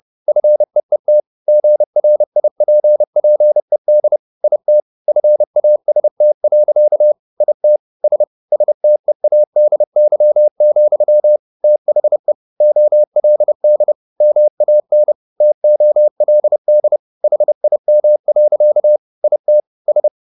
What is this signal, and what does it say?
Morse code, heard as a keyed tone, about 30 wpm, 600 Hz. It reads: FEET GRIPPED IT FAST. IT S STEADY, THE OLD MAN TOLD HIM. IT S
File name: the old man and the sea / the old man and the sea 1 30wpm